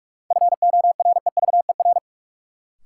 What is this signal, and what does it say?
Morse code, heard as a keyed tone, about 46 wpm, 700 Hz. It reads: FOREVER
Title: Sentences from Top 1600 Words - New Word Drill - 45wpm